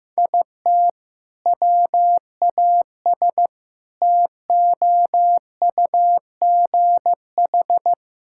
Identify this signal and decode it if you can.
Morse code, heard as a keyed tone, about 15 wpm, 700 Hz. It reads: IT WAS TOUGH
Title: Sentences from Top 1600 Words 15wpm